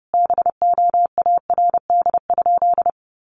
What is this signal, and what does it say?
Morse code, heard as a keyed tone, about 30 wpm, 700 Hz. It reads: BOARD?
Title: Sentences from Top 1300 Words 30wpm